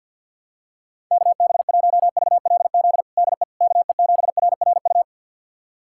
Morse code, heard as a keyed tone, about 52 wpm, 700 Hz. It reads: KB1VLZ DE KE7RRU
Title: USA Callsign DE Callsign 50wpm